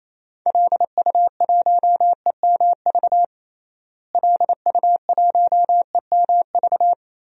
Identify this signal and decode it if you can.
Morse code, heard as a keyed tone, about 28 wpm, 700 Hz. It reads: LU1EMV LU1EMV
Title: DXCC-100 Call Signs - 6 Characters - Repeated 2x - 28wpm